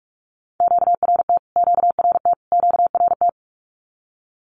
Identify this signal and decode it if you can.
Morse code, heard as a keyed tone, about 44 wpm, 700 Hz. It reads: QRT QRT QRT
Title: Q-Codes Repeated 3x 45wpm